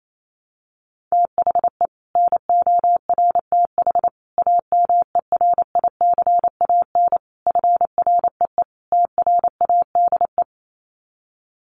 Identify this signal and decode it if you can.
Morse code, heard as a keyed tone, about 28 wpm, 700 Hz. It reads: THE NORTH AMERICAN FREE TRADE
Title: Sets of 5 Words Encore 28wpm